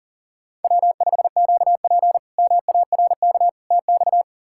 Morse code, heard as a keyed tone, about 37 wpm, 700 Hz. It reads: W5QP MARK TX
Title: K1USN SST QSO Practice - 40wpm